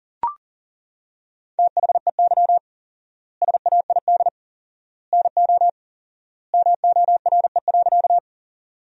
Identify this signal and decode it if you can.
Morse code, heard as a keyed tone, about 40 wpm, 700 Hz. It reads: THEY SAID NO MORE.
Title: 4x Word-Spacing - Sentences from Top 100 Words - 40wpm